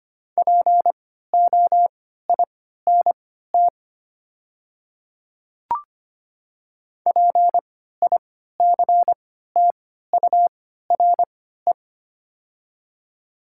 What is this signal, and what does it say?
Morse code, heard as a keyed tone, about 25 wpm, spaced as for 14 wpm, 700 Hz. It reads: POINT PICTURE
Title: 3x Character Spacing - Top 300 Words - 25wpm